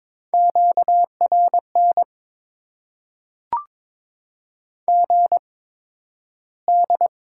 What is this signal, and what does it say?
Morse code, heard as a keyed tone, about 22 wpm, spaced as for 12 wpm, 700 Hz. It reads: QRN GD
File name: Top 90 QSO Elements - 8x Character Spacing - 22wpm